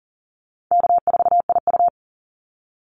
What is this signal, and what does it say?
Morse code, heard as a keyed tone, about 40 wpm, 700 Hz. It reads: K4IU
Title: Call Signs - 40wpm